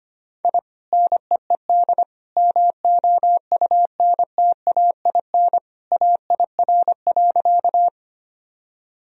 Morse code, heard as a keyed tone, about 25 wpm, 700 Hz. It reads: I NEED MOUNTAIN AIR.
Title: Sentences from Top 1000 Words - Review Part 1 - 25wpm